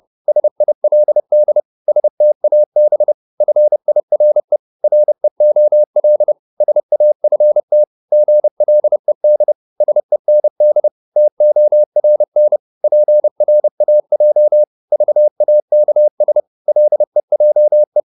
Morse code, heard as a keyed tone, about 30 wpm, 600 Hz. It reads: SILD STAB FIRE REOL SAFT GLED SEND TORN PRAJ VAKS LEJE
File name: Korte ord DK 30wpm